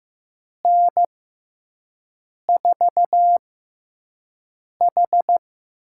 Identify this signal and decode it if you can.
Morse code, heard as a keyed tone, about 15 wpm, spaced as for 5 wpm, 700 Hz. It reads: N4H
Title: USA Call Signs - 6 Characters - 6x Character Spacing - 15wpm